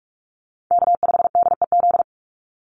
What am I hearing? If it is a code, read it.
Morse code, heard as a keyed tone, about 44 wpm, 700 Hz. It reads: K5DEZ
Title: NA SOTA Chaser Callsigns 45wpm